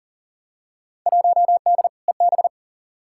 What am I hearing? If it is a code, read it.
Morse code, heard as a keyed tone, about 38 wpm, 700 Hz. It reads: 1D EB
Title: Field Day Exchanges - 40wpm